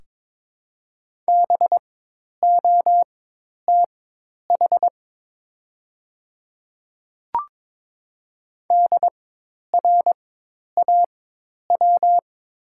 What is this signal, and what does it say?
Morse code, heard as a keyed tone, about 22 wpm, spaced as for 10 wpm, 700 Hz. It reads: BOTH DRAW
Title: 4x Character Spacing - Top 300 Words - 22wpm